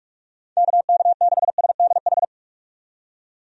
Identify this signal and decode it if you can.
Morse code, heard as a keyed tone, about 47 wpm, 700 Hz. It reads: KK6SDH
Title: USA Call Signs - 6 Characters - 45wpm